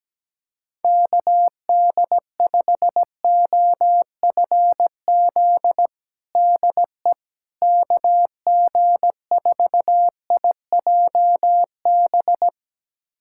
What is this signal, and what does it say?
Morse code, heard as a keyed tone, about 17 wpm, 700 Hz. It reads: KD5OFZ DE KG4IJB
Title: USA Callsign DE Callsign - 6 Characters - 17wpm